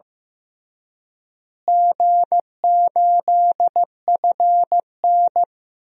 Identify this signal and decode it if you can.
Morse code, heard as a keyed tone, about 15 wpm, 700 Hz. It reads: G8FN
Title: DXCC-100 Call Signs - 4 Characters - Repeated 3x - 15wpm